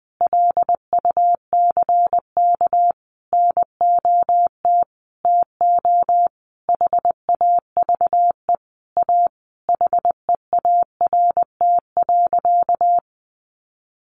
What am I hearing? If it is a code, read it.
Morse code, heard as a keyed tone, about 20 wpm, 700 Hz. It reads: LUCK NOT TO HAVE A HEART.
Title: Sentences from Top 1500 Words 20wpm